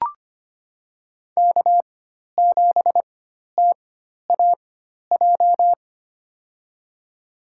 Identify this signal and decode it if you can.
Morse code, heard as a keyed tone, about 25 wpm, spaced as for 12 wpm, 700 Hz. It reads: K7TAJ
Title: USA Call Signs - 5 Characters - 4x Character Spacing - 25wpm